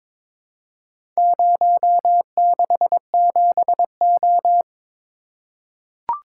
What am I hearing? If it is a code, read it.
Morse code, heard as a keyed tone, about 22 wpm, 700 Hz. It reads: 067O
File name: Four Letter-Number Combinations - 22wpm